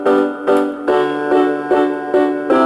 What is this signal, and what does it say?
There is no code.